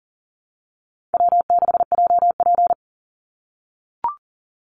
Morse code, heard as a keyed tone, about 40 wpm, 700 Hz. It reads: W6JP
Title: NA SOTA Chaser Callsigns 40wpm